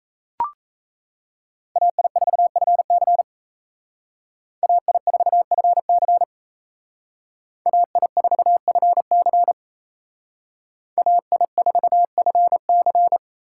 Morse code, heard as a keyed tone, about 35 wpm, 700 Hz. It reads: AI4FC AI4FC AI4FC AI4FC